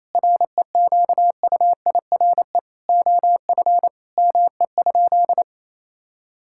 Morse code, heard as a keyed tone, about 28 wpm, 700 Hz. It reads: REQUIRE OF ME?
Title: Sentences from Top 1000 Words - Review Part 6 - 28wpm